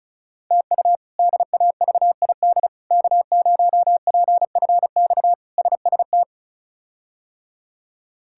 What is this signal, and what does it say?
Morse code, heard as a keyed tone, about 36 wpm, 700 Hz. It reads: TU DAVID K0PFX SST
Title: K1USN SST QSO Practice - 35wpm